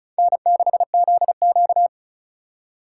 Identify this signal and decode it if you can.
Morse code, heard as a keyed tone, about 44 wpm, 700 Hz. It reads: N6ZQ